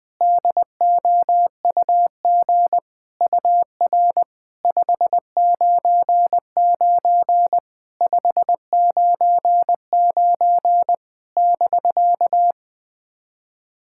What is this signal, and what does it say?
Morse code, heard as a keyed tone, about 20 wpm, 700 Hz. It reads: DOUG UR 599 599 <BK>
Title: SOTA QSO Practice 20wpm